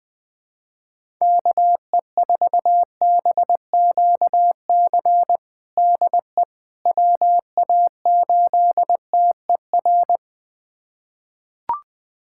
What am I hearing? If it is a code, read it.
Morse code, heard as a keyed tone, about 20 wpm, 700 Hz. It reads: KE4BQC DE WA8TER
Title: USA Callsign DE Callsign 20wpm